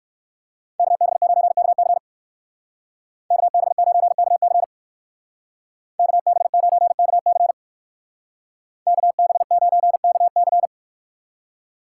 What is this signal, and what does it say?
Morse code, heard as a keyed tone, about 55 wpm, 700 Hz. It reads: KB9KC KB9KC KB9KC KB9KC
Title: Speed Racing - USA 2x2 Call Signs 45wpm